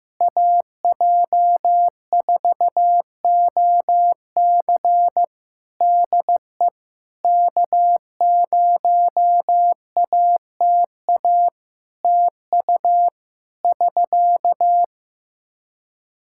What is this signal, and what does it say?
Morse code, heard as a keyed tone, about 15 wpm, 700 Hz. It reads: AJ4OC DE K0ATA TU <SK>
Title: Practice QSO - Exchange 4 - Ending - 15wpm